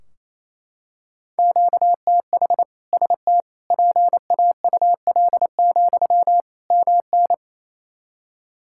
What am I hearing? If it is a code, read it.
Morse code, heard as a keyed tone, about 27 wpm, 700 Hz. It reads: QTH ST PAUL, MN
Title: K1USN SST QSO Practice 28wpm